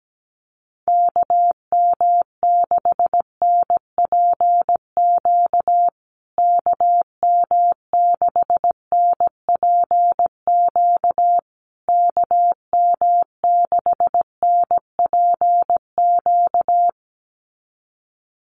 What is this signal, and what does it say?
Morse code, heard as a keyed tone, about 17 wpm, 700 Hz. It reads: KM6NPQ KM6NPQ KM6NPQ
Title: USA Call Signs Repeated 3x 17wpm